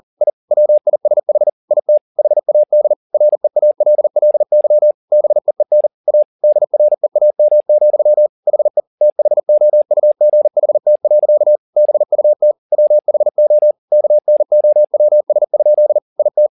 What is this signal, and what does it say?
Morse code, heard as a keyed tone, about 40 wpm, 600 Hz. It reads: I WISH IT HAD REALLY BEEN A DREAM, HE THOUGHT. BUT WHO KNOWS? IT
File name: the old man and the sea 2 40wpm